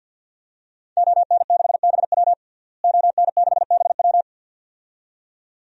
Morse code, heard as a keyed tone, about 48 wpm, 700 Hz. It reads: ON6BW ON6BW
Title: DXCC-100 Call Signs - 5 Characters - Repeated 2x - 50wpm